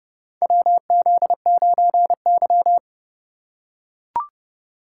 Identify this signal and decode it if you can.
Morse code, heard as a keyed tone, about 30 wpm, 700 Hz. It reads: WZ9Y
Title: USA Call Signs - 4 Characters - 4x Character Spacing - 30wpm